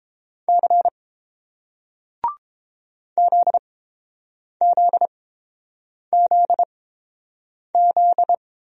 Morse code, heard as a keyed tone, about 30 wpm, spaced as for 7 wpm, 700 Hz. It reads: C ZZZZ